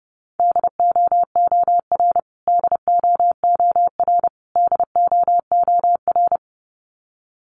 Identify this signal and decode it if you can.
Morse code, heard as a keyed tone, about 30 wpm, 700 Hz. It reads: DOOR DOOR DOOR
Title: Top 500 Words Repeated 30wpm